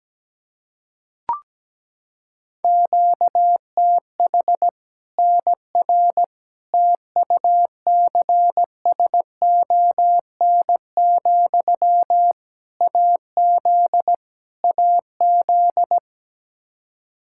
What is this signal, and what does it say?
Morse code, heard as a keyed tone, about 17 wpm, 700 Hz. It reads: QTH NR TUCSON, AZ AZ
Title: Practice QSO - Exchange 1 - Beginning - 17wpm